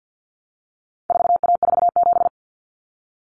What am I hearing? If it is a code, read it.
Morse code, heard as a keyed tone, about 50 wpm, 700 Hz. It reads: 4A47
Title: Common Cut Numbers - 4-Digit 50wpm